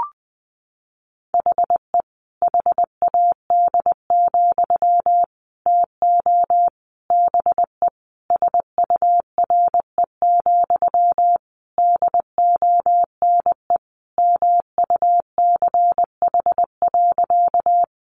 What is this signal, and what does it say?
Morse code, heard as a keyed tone, about 20 wpm, 700 Hz. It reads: HE HAD, TO BE SURE, DONE MUCH.
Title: Sentences from Top 500 Words 20wpm